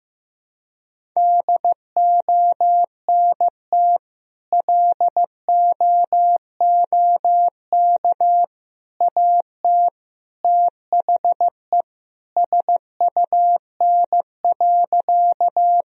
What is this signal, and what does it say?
Morse code, heard as a keyed tone, about 16 wpm, 700 Hz. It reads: DONT LOOK AT THE SUN.
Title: Sentences from Top 1000 Words - Review Part 1 - 15wpm